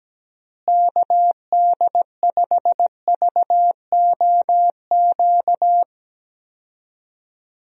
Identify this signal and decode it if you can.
Morse code, heard as a keyed tone, about 17 wpm, 700 Hz. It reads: KD5VOQ
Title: USA Call Signs - 6 Characters - Repeated 2x - 17wpm